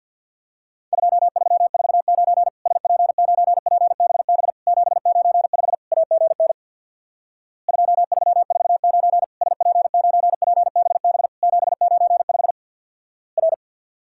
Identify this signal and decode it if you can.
Morse code, heard as a keyed tone, about 50 wpm, 700 Hz. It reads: 2349 SP9PBB 795 AGN 2349 SP9PBB 795 R
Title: Worked All Europe DX Contest - QTC Exchange with Repeats - 50wpm